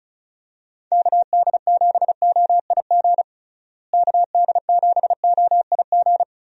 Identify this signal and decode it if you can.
Morse code, heard as a keyed tone, about 34 wpm, 700 Hz. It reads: KD7OIG KD7OIG
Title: USA Call Signs - 6 Characters - Repeated 3x - 35wpm